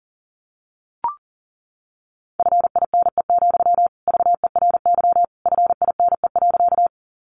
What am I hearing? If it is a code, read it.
Morse code, heard as a keyed tone, about 37 wpm, 700 Hz. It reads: FINE, VERY FINE.
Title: Sentences from Top 500 Words 40wpm